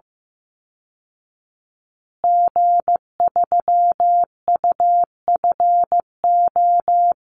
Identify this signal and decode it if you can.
Morse code, heard as a keyed tone, about 15 wpm, 700 Hz. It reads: G3UFO
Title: Call Signs Repeated 3x - 15wpm